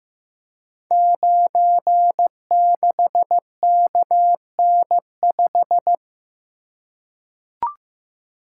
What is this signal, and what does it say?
Morse code, heard as a keyed tone, about 15 wpm, 700 Hz. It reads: 96KN5
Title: Code Groups - 5 Letter:Numbers - 15wpm